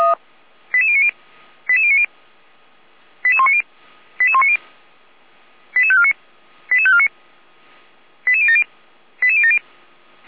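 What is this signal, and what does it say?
Five-tone selective call (ZVEI), heard as quick runs of five tones, tones of about 70 ms. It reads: 88090 88090 88190 88190 88549 88549 88070 88070
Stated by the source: Five-tone ZVEI selective call received by SDR, from feuerwehr-orlamuende.org audio archive (fire brigade Orlamünde).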